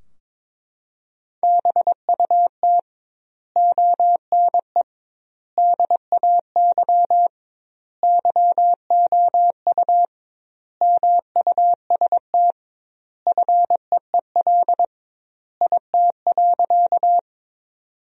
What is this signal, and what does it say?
Morse code, heard as a keyed tone, about 22 wpm, 700 Hz. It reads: BUT ONE DAY YOU MUST FEEL IT.
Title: 2x Word-Spacing - Sentences from Top 400 Words - 22wpm